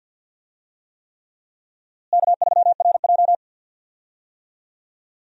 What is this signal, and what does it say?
Morse code, heard as a keyed tone, about 52 wpm, 700 Hz. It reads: K3RJ
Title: USA Call Signs - 4 Characters - Repeated 3x - 50wpm